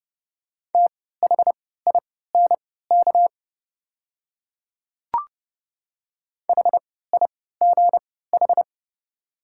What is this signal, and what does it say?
Morse code, heard as a keyed tone, about 32 wpm, spaced as for 17 wpm, 700 Hz. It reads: THINK HIGH
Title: 3x Character Spacing - Top 300 Words - 30wpm